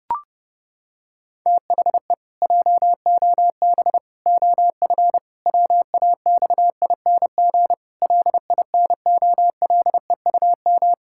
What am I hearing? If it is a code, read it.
Morse code, heard as a keyed tone, about 30 wpm, 700 Hz. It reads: THE JOB OF WAXING LINOLEUM
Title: Pangram Challenge v2 - 30wpm